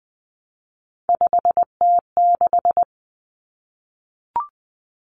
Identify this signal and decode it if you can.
Morse code, heard as a keyed tone, about 20 wpm, 700 Hz. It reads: 5T6
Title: Common Cut Numbers - 3-Digit 20wpm